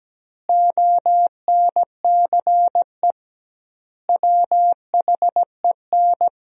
Morse code, heard as a keyed tone, about 17 wpm, 700 Hz. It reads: ONCE WHEN
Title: 2x Word-Spacing - Sentences from Top 300 Words - 17wpm